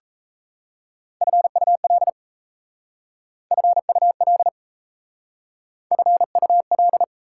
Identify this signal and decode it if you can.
Morse code, heard as a keyed tone, about 34 wpm, 700 Hz. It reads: FUL FUL FUL
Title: Speed-Racing Prefixes and Suffixes 28wpm